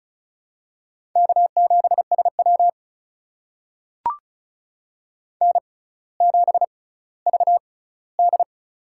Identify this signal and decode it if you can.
Morse code, heard as a keyed tone, about 36 wpm, spaced as for 18 wpm, 700 Hz. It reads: K7SW N7VD
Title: USA Call Signs - 4 Characters - 6x Character Spacing - 35wpm